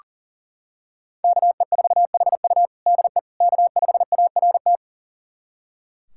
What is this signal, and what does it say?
Morse code, heard as a keyed tone, about 40 wpm, 700 Hz. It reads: KE4HU DE K5ART
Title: USA Callsign DE Callsign - 5 Characters - 40wpm